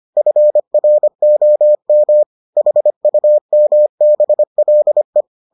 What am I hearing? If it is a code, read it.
Morse code, heard as a keyed tone, about 25 wpm, 600 Hz. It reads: FROM HUMBLE